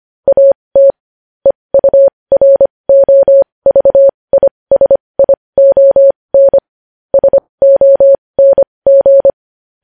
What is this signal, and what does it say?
Morse code, heard as a keyed tone, about 25 wpm, spaced as for 20 wpm, 550 Hz. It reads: AT EUROVISION SONG
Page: news-headlines-in-morse-code-at-20-wpm